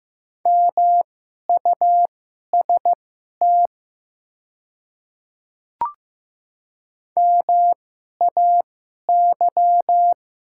Morse code, heard as a keyed tone, about 15 wpm, spaced as for 11 wpm, 690 Hz. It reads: MUST MAY
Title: 2x Character Spacing - Top 200 Words - 15wpm